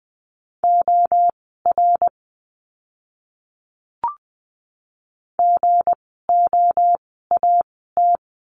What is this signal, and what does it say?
Morse code, heard as a keyed tone, about 20 wpm, spaced as for 14 wpm, 700 Hz. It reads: OR GOAT
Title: 160 Lesson 26f - Letter g - Words 20wpm